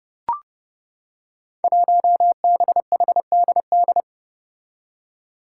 Morse code, heard as a keyed tone, about 31 wpm, 700 Hz. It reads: 1BHDD
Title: Five Challenging Letter-Number Combinations 30wpm